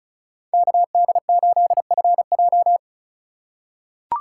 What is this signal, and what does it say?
Morse code, heard as a keyed tone, about 36 wpm, 700 Hz. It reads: KD8FJ
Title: USA Call Signs - 4x Character Spacing - 35wpm